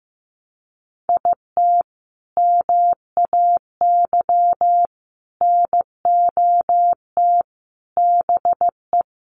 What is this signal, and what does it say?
Morse code, heard as a keyed tone, about 15 wpm, 700 Hz. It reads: IT MAY NOT BE
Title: Sentences from Top 1400 Words 15wpm